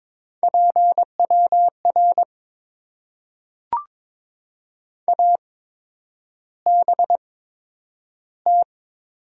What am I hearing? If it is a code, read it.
Morse code, heard as a keyed tone, about 23 wpm, spaced as for 10 wpm, 690 Hz. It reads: PWR ABT
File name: Top 90 QSO Elements - 8x Character Spacing - 22wpm